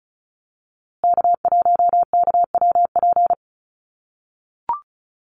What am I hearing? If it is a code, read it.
Morse code, heard as a keyed tone, about 34 wpm, 700 Hz. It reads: K1KWP
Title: USA Call Signs Repeated 3x 35wpm